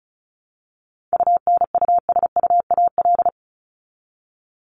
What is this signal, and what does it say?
Morse code, heard as a keyed tone, about 34 wpm, 700 Hz. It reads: UNUSUAL
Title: Top 1251-1500 Words 35wpm